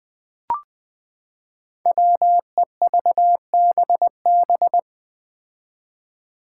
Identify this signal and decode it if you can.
Morse code, heard as a keyed tone, about 20 wpm, 700 Hz. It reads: WEVBB
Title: Five Challenging Letter-Number Combinations 20wpm